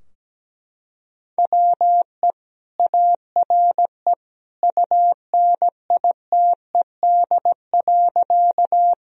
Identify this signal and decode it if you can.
Morse code, heard as a keyed tone, about 17 wpm, 700 Hz. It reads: WE ARE UNITED.
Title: Sentences from Top 1000 Words - Review Part 4 - 17wpm